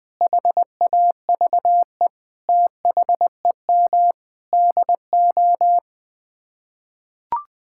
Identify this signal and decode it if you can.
Morse code, heard as a keyed tone, about 20 wpm, 700 Hz. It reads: HAVE THEM DO
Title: Sets of 9 Words - 20wpm